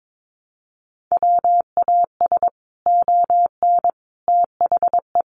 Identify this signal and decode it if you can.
Morse code, heard as a keyed tone, about 22 wpm, 700 Hz. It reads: WAS ON THE
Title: Sets of 5 Words Encore 22wpm